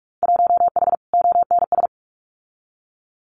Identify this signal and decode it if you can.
Morse code, heard as a keyed tone, about 44 wpm, 700 Hz. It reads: 1H ONS
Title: Winter Field Day Exchanges 45wpm